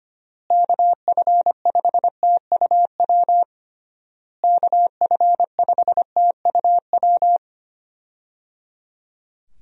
Morse code, heard as a keyed tone, about 25 wpm, 700 Hz. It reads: KF5TUW KF5TUW
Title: USA Call Signs - 6 Characters - Repeated 3x - 25wpm